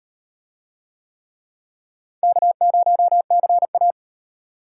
Morse code, heard as a keyed tone, about 38 wpm, 700 Hz. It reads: K0CA